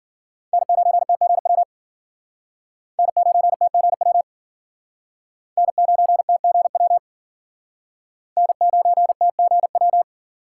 Morse code, heard as a keyed tone, about 53 wpm, 700 Hz. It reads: N9TGW N9TGW N9TGW N9TGW